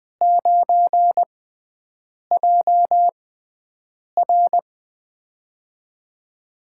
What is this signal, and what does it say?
Morse code, heard as a keyed tone, about 20 wpm, spaced as for 7 wpm, 700 Hz. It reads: 9JR
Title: USA Call Signs - 4 Characters - 6x Character Spacing - 20wpm